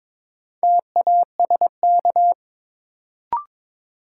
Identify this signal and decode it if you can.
Morse code, heard as a keyed tone, about 22 wpm, 700 Hz. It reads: TASK
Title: Sentences from Top 1000 Words - Review Part 7 - New Word Drill - 22wpm